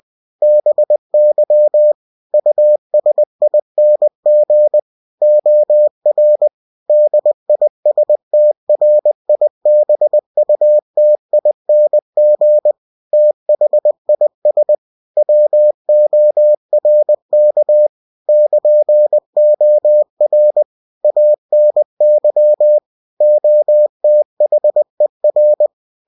Morse code, heard as a keyed tone, about 20 wpm, 600 Hz. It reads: BY USING OR DISTRIBUTING THIS WORK <KN>OR ANY OTHER
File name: cw-books / winnie-the-pooh / winnie-20wpm-page-0113